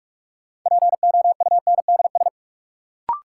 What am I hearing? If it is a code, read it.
Morse code, heard as a keyed tone, about 46 wpm, 700 Hz. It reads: POUNDS
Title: Sentences from Top 1000 Words - Review Part 2 - New Word Drill - 45wpm